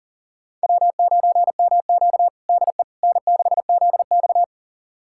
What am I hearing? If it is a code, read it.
Morse code, heard as a keyed tone, about 40 wpm, 700 Hz. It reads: W9MQ DE N6ZX
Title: USA Callsign DE Callsign - 4 Characters - 40wpm